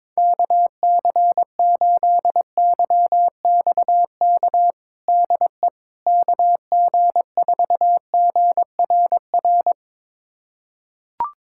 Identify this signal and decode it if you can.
Morse code, heard as a keyed tone, about 22 wpm, 700 Hz. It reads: KC8YXK DE KG4GRR
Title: USA Callsign DE Callsign - 6 Characters - 22wpm